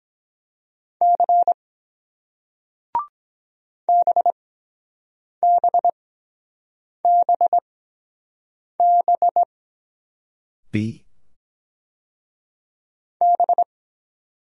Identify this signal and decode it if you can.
Morse code, heard as a keyed tone, about 23 wpm, spaced as for 7 wpm, 700 Hz. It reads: C BBBB B